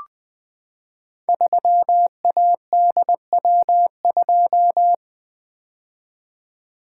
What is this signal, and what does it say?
Morse code, heard as a keyed tone, about 20 wpm, 700 Hz. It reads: 3ADW2
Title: Code Groups - 5 Letter:Numbers - 20wpm